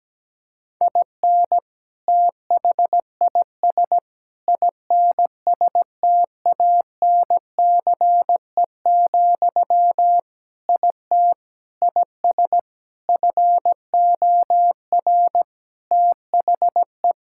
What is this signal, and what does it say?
Morse code, heard as a keyed tone, about 17 wpm, 700 Hz. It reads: IN THIS INSTANCE, IT IS FOR THE
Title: Sentences from Top 1000 Words - Review Part 5 - 17wpm